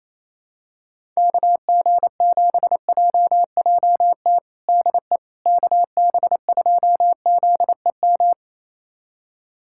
Morse code, heard as a keyed tone, about 27 wpm, 700 Hz. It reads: KG7JJT DE KB2ZEM
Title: USA Callsign DE Callsign - 6 Characters - 28wpm